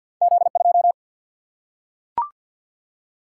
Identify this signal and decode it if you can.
Morse code, heard as a keyed tone, about 53 wpm, 700 Hz. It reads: Z 2
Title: Single-Letter Single-Number 50wpm